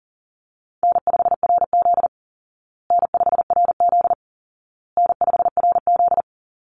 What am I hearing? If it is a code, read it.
Morse code, heard as a keyed tone, about 37 wpm, 700 Hz. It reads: N5RZ N5RZ N5RZ
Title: Call Signs Repeated 2x - 40wpm